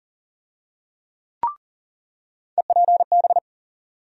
Morse code, heard as a keyed tone, about 37 wpm, 700 Hz. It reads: EPB
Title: USA Call Sign Postfixes - 3 Letters 40wpm